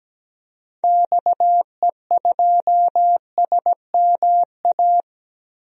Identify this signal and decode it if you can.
Morse code, heard as a keyed tone, about 17 wpm, 700 Hz. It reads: XE2SMA
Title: DXCC-100 Call Signs - 6 Characters - 17wpm